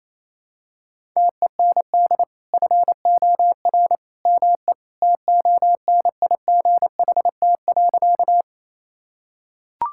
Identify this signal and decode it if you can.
Morse code, heard as a keyed tone, about 28 wpm, 700 Hz. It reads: TEND FOR ME TONIGHT.
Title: Sentences from Top 1000 Words - Review Part 5 - 28wpm